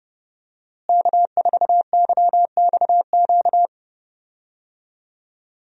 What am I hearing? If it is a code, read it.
Morse code, heard as a keyed tone, about 32 wpm, 700 Hz. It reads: K4YXQ
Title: USA Call Signs - 5 Characters - Repeated 3x - 30wpm